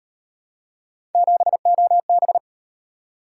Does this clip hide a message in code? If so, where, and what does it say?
Yes: Morse code from 1.1 to 2.4 s, heard as a keyed tone, about 38 wpm, 700 Hz. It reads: 7OB